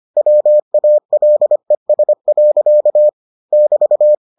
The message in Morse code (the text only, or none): WALES. <BT>